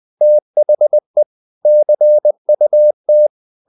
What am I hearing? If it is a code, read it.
Morse code, heard as a keyed tone, about 20 wpm, 600 Hz. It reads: THE CUT
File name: the old man and the sea part 2 20wpm